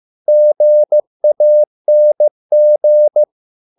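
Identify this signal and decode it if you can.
Morse code, heard as a keyed tone, about 15 wpm, 600 Hz. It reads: GANG
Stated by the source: Morse code at a slow 15 wpm from CW med Gnister.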